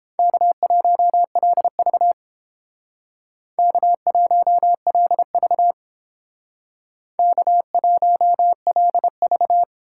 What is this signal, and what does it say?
Morse code, heard as a keyed tone, about 28 wpm, 700 Hz. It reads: K1LV K1LV K1LV